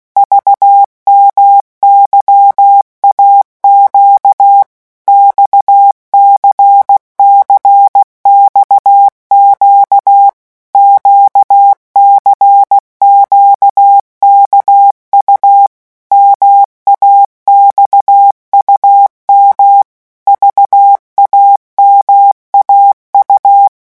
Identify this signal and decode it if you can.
Morse code, heard as a keyed tone, about 16 wpm, 800 Hz. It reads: VMYAQ XCCXQ QCQKU MAXUM VAMAU
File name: morse mp3 / revision2-vitesse 15 mots